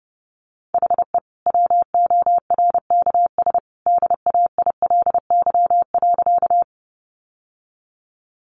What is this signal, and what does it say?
Morse code, heard as a keyed tone, about 30 wpm, 700 Hz. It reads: HE WORKS DAILY.
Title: Sentences from Top 1100 Words 30wpm